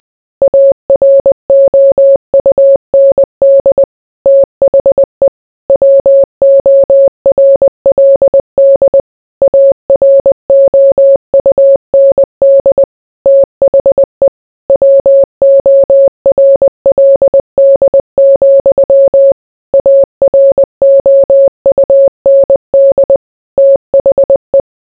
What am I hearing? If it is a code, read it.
Morse code, heard as a keyed tone, about 20 wpm, 550 Hz. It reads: AROUND THE WORLD AROUND THE WORLD, AROUND THE